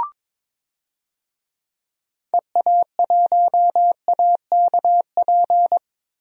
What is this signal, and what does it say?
Morse code, heard as a keyed tone, about 22 wpm, 700 Hz. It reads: EA1AKP
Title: DXCC-100 Call Signs - 6 Characters - Repeated 3x - 22wpm